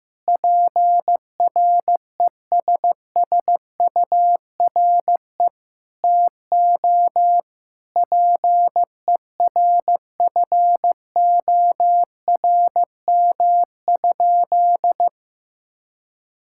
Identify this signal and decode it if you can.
Morse code, heard as a keyed tone, about 15 wpm, 700 Hz. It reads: PRESSURE TO PERFORM?
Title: Sentences from Top 1000 Words - Review Part 6 - 15wpm